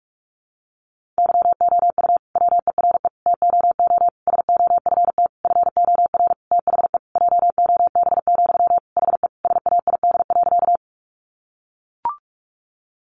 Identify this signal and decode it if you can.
Morse code, heard as a keyed tone, about 44 wpm, 700 Hz. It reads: YOU WERE TOO SOFT FOR THE JOB, HE SAID.
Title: Sentences from Top 1500 Words 45wpm